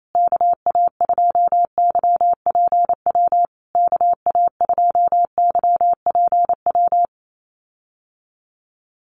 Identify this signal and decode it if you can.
Morse code, heard as a keyed tone, about 28 wpm, 700 Hz. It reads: KA2YPW KA2YPW
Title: USA Call Signs Repeated 3x 28wpm